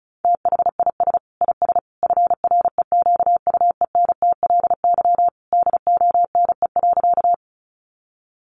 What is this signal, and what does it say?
Morse code, heard as a keyed tone, about 34 wpm, 700 Hz. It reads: THIS IS FREQUENTLY DONE.
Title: Sentences from Top 1300 Words 35wpm